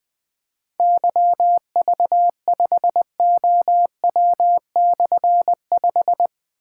Morse code, heard as a keyed tone, about 21 wpm, 700 Hz. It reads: YV5OW/5
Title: Difficult Call Signs 20wpm